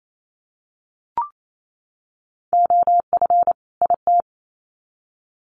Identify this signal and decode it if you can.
Morse code, heard as a keyed tone, about 28 wpm, 700 Hz. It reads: OF IT
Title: Sets of 2 Words 28wpm